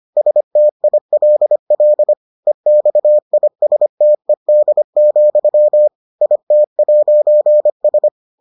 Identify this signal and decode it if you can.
Morse code, heard as a keyed tone, about 25 wpm, 600 Hz. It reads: STILL EXISTED, IT'S